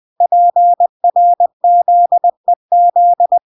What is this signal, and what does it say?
Morse code, heard as a keyed tone, about 20 wpm, 700 Hz. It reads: PRZEZ